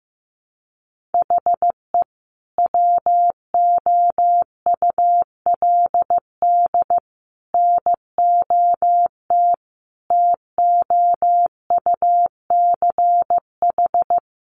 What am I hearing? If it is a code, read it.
Morse code, heard as a keyed tone, about 15 wpm, 700 Hz. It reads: HE WOULD NOT TOUCH
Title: Sentences from Top 1200 Words 15wpm